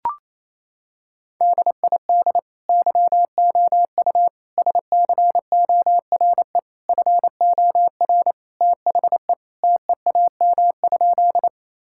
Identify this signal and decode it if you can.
Morse code, heard as a keyed tone, about 28 wpm, 700 Hz. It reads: DID YOU SCORE FOR THE TEAM?
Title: Sentences from Top 1600 Words 28wpm